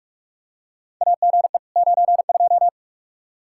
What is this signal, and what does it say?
Morse code, heard as a keyed tone, about 46 wpm, 700 Hz. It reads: AGE 92
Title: Practice QSO - Exchange 3 - More Info - 45wpm